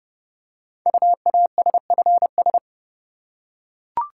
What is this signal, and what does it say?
Morse code, heard as a keyed tone, about 30 wpm, 700 Hz. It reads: UASFS
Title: Code Groups - 5 Letters - 30wpm